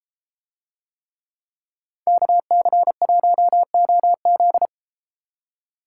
Morse code, heard as a keyed tone, about 31 wpm, 700 Hz. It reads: KC1OZ